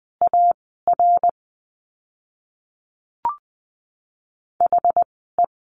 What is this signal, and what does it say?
Morse code, heard as a keyed tone, about 20 wpm, spaced as for 15 wpm, 700 Hz. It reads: AR HE
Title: 206 Lesson 33f - Letter j - Words 20wpm